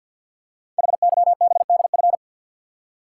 Morse code, heard as a keyed tone, about 53 wpm, 700 Hz. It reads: HYBDF
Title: Code Groups - 5 Letter:Numbers - 50wpm